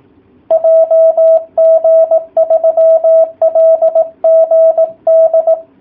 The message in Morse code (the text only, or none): JG3LGD